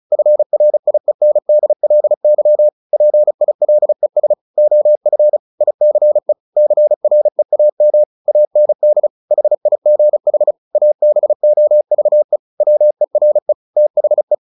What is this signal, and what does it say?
Morse code, heard as a keyed tone, about 34 wpm, 600 Hz. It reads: FRIENDLY PILES OF ICE CREAM AND HIGH ABOVE WERE THE
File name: the old man and the sea / the old man and the sea 1 35wpm